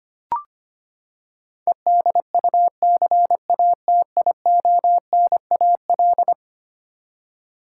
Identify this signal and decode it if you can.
Morse code, heard as a keyed tone, about 25 wpm, 700 Hz. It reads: EDUCATIONAL
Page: morse-code-ninja-practice-25wpm